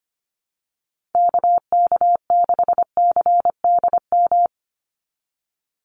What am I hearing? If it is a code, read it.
Morse code, heard as a keyed tone, about 25 wpm, 700 Hz. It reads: KK6CDM
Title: USA Call Signs 25wpm